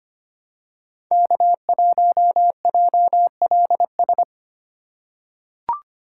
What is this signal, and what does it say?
Morse code, heard as a keyed tone, about 25 wpm, 700 Hz. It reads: K1JLS
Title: USA Call Signs - 5 Characters - 2x Character Spacing - 25wpm